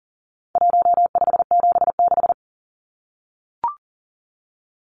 Morse code, heard as a keyed tone, about 40 wpm, 700 Hz. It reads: 1576
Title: Numbers - Contest Style - 4-Digit 40wpm